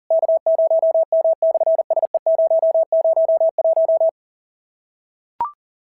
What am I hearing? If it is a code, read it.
Morse code, heard as a keyed tone, about 40 wpm, 650 Hz. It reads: K0M/SE001
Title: SOTA QSO Practice 40wpm